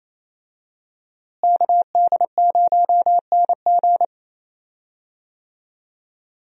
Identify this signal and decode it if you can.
Morse code, heard as a keyed tone, about 27 wpm, 700 Hz. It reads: KD0NG
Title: USA Call Signs - 5 Characters - Repeated 3x - 28wpm